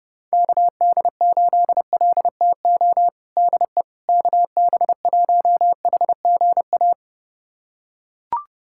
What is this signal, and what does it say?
Morse code, heard as a keyed tone, about 30 wpm, 700 Hz. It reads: KD8LTO DE KB1HGA
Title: USA Callsign DE Callsign - 6 Characters - 30wpm